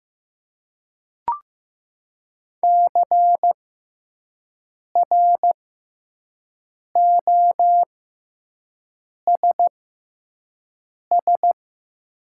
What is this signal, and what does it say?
Morse code, heard as a keyed tone, about 15 wpm, spaced as for 5 wpm, 700 Hz. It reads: CROSS
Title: 6x Character Spacing - Top 300 Words - 15wpm